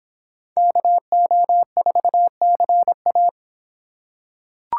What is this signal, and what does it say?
Morse code, heard as a keyed tone, about 26 wpm, 700 Hz. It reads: KO4CA